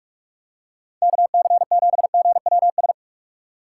Morse code, heard as a keyed tone, about 44 wpm, 700 Hz. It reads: KC7GWS